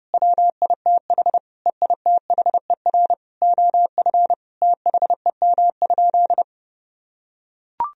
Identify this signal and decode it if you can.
Morse code, heard as a keyed tone, about 32 wpm, 700 Hz. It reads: WITH EITHER OF THEM?
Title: Sentences from Top 1000 Words - Review Part 4 - 30wpm